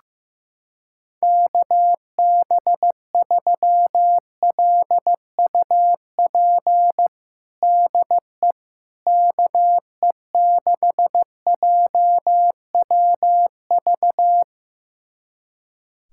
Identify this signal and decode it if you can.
Morse code, heard as a keyed tone, about 15 wpm, 700 Hz. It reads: KB3LUP DE KE6JWV
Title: USA Callsign DE Callsign 15wpm